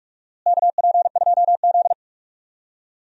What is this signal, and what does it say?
Morse code, heard as a keyed tone, about 46 wpm, 700 Hz. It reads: KP2Z
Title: DXCC-100 Call Signs - 4 Characters - Repeated 3x - 45wpm